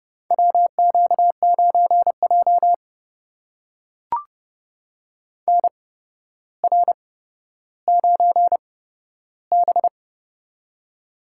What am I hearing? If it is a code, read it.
Morse code, heard as a keyed tone, about 30 wpm, spaced as for 13 wpm, 700 Hz. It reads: WQ9J NR9B
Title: USA Call Signs - 4 Characters - 8x Character Spacing - 30wpm